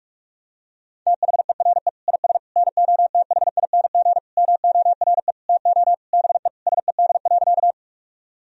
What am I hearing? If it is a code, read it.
Morse code, heard as a keyed tone, about 46 wpm, 690 Hz. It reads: THERE IS NOTHING MORE TO BE SED.
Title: Sentences from Top 500 Words and Abbreviations 45wpm